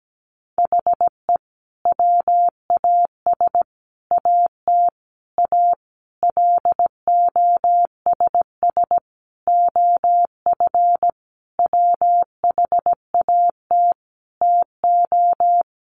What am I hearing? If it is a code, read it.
Morse code, heard as a keyed tone, about 17 wpm, 700 Hz. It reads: HE WAS AT A LOSS OF WHAT TO
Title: Sentences from Top 1400 Words 17wpm